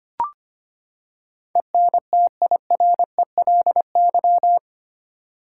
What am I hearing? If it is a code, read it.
Morse code, heard as a keyed tone, about 25 wpm, 700 Hz. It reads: ENTIRELY
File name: Eight Letter Words - 25wpm